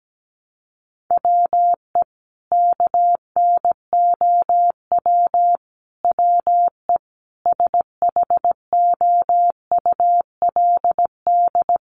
Morse code, heard as a keyed tone, about 17 wpm, 700 Hz. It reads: WE KNOW WE SHOULD
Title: Sentences from Top 1100 Words 17wpm